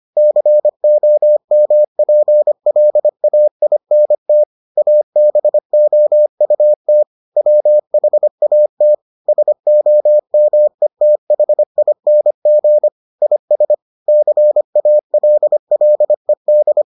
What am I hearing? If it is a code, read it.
Morse code, heard as a keyed tone, about 25 wpm, 600 Hz. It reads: COMPLAINT ABOUT WHAT SOMETHING IS CALLED